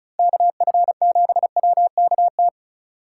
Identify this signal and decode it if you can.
Morse code, heard as a keyed tone, about 43 wpm, 700 Hz. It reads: KF7WKT
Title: Speed Racing - USA Call Signs 35wpm